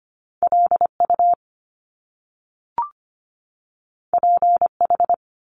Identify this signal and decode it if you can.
Morse code, heard as a keyed tone, about 25 wpm, 700 Hz. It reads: LU PH
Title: Top Two Letter Combinations in Words 25wpm